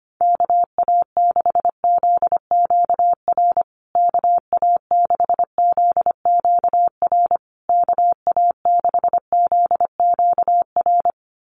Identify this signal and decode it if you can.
Morse code, heard as a keyed tone, about 25 wpm, 700 Hz. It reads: KA6ZQR KA6ZQR KA6ZQR
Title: USA Call Signs Repeated 3x 25wpm